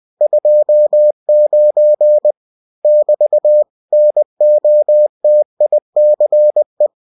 Morse code, heard as a keyed tone, about 20 wpm, 600 Hz. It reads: 29 <BT> NOTICE
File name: cw-books / winnie-the-pooh / winnie-20wpm-page-0029